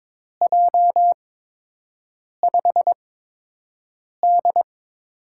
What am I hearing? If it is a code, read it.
Morse code, heard as a keyed tone, about 22 wpm, spaced as for 6 wpm, 700 Hz. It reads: J5D
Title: USA Call Signs - 4 Characters - 8x Character Spacing - 22wpm